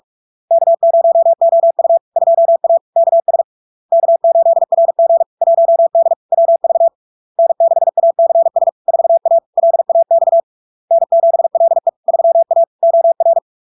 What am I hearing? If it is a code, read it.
Morse code, heard as a keyed tone, about 46 wpm, 670 Hz. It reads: K0OU 2A KS K8RG 1D WV N6ACS 4A LAX N7LE 3A OR
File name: LARK-FD-2022-Log-in-CW-45wpm-670-Hz